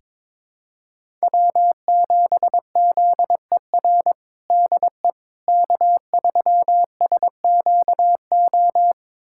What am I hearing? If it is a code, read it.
Morse code, heard as a keyed tone, about 22 wpm, 700 Hz. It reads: W7ZER DE K3SQO
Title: USA Callsign DE Callsign - 5 Characters - 22wpm